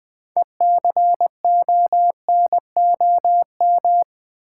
Morse code, heard as a keyed tone, about 20 wpm, 700 Hz. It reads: ECONOM
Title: Build A Word - 3 Letters At A Time - 9-Letter Words - 20wpm